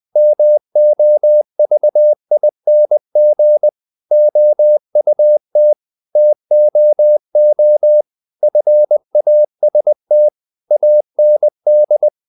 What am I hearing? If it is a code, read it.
Morse code, heard as a keyed tone, about 20 wpm, 600 Hz. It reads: MOVING OUT TOO FAST AND
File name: the old man and the sea / the old man and the sea part 1 20wpm